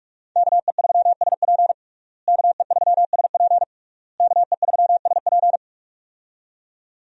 Morse code, heard as a keyed tone, about 41 wpm, 700 Hz. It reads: KE3SP KE3SP KE3SP
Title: USA Call Signs - 5 Characters - Repeated 3x - 45wpm